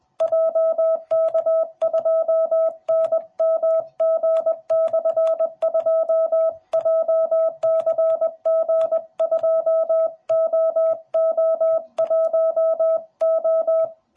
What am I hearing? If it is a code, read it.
Morse code, heard as a keyed tone, about 20 wpm, 650 Hz. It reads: JK2NMG/2 JCG2OO1O